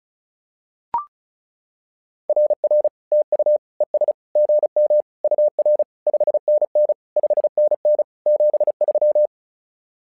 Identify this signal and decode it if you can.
Morse code, heard as a keyed tone, about 36 wpm, 600 Hz. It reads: RR TU ES GM UR 5NN 5NN 73
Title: SOTA QSO Practice 35wpm